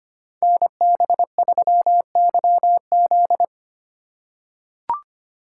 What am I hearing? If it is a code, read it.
Morse code, heard as a keyed tone, about 25 wpm, 700 Hz. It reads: NB3YZ
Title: Code Groups - 5 Letter:Numbers - 25wpm